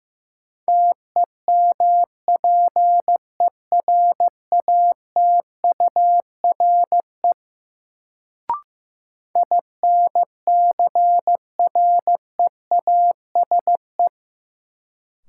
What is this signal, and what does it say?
Morse code, heard as a keyed tone, about 15 wpm, 700 Hz. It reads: TEMPERATURE INCREASE
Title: Sentences from Top 1000 Words - Review Part 4 - New Word Drill - 15wpm